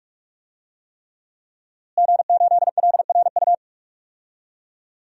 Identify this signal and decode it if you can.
Morse code, heard as a keyed tone, about 46 wpm, 700 Hz. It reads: G8LRU
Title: DXCC-100 Call Signs - 5 Characters - Repeated 3x - 45wpm